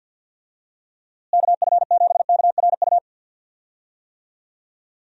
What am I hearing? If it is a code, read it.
Morse code, heard as a keyed tone, about 48 wpm, 700 Hz. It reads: KF7KRU
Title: USA Call Signs - 6 Characters - Repeated 3x - 50wpm